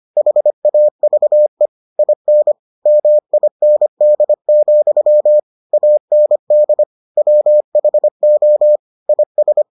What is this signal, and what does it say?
Morse code, heard as a keyed tone, about 25 wpm, 600 Hz. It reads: HAVE IN MIND, AND WHO IS